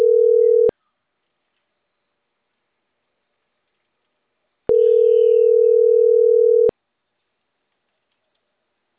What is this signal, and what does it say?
Ringback tone (North America), 440 + 480 Hz.